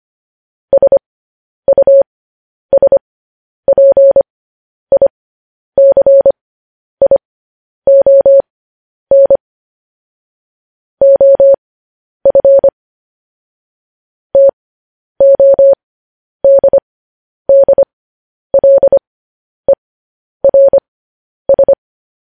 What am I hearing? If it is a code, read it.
Morse code, heard as a keyed tone, about 25 wpm, spaced as for 10 wpm, 550 Hz. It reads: SUSPICION OF TODDLERS
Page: news-headlines-in-morse-code-at-10-wpm